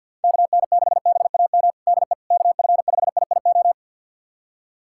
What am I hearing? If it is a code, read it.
Morse code, heard as a keyed tone, about 49 wpm, 700 Hz. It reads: KN6BAM DE KF5IIO
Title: USA Callsign DE Callsign - 6 Characters - 50wpm